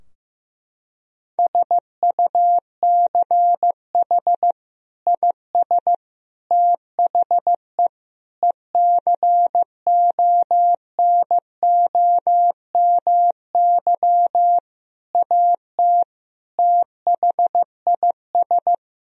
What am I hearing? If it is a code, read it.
Morse code, heard as a keyed tone, about 16 wpm, 700 Hz. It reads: SUCH IS THE ECONOMY AT THIS
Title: Sentences from Top 1000 Words - Review Part 3 - 15wpm